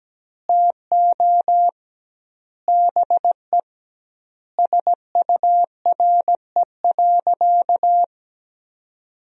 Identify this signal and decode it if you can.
Morse code, heard as a keyed tone, about 17 wpm, 700 Hz. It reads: TO BE SURE.
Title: 2x Word-Spacing - Sentences from Top 500 Words - 17wpm